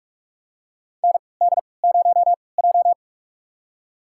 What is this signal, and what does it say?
Morse code, heard as a keyed tone, about 46 wpm, spaced as for 25 wpm, 700 Hz. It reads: ND0J
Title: USA Call Signs - 4 Characters - 3x Character Spacing - 45wpm